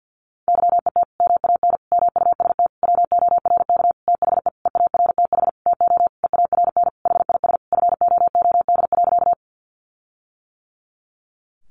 Morse code, heard as a keyed tone, about 50 wpm, 700 Hz. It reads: YET MAN MUST WORK THE EARTH TO EARN HIS FOOD.
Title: Sentences from Top 1500 Words 50wpm